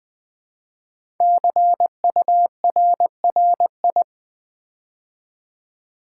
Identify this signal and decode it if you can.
Morse code, heard as a keyed tone, about 20 wpm, 700 Hz. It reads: CURRI
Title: Build A Word - 5 Letters At A Time - 10-Letter Words - 20wpm